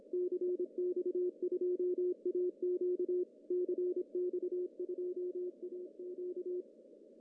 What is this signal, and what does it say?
Morse code, heard as a keyed tone, about 26 wpm, 360 Hz, with radio noise. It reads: CX2AQ CX2AQ
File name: CW / cqww-cw-k5tr